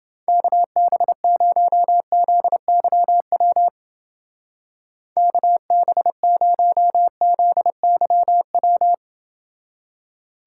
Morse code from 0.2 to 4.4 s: KB0ZYW

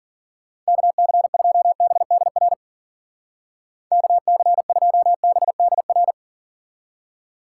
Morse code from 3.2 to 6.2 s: KC2BDR

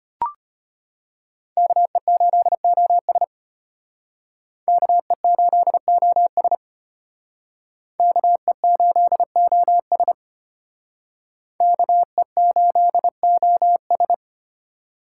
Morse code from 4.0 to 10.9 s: KE8OS KE8OS

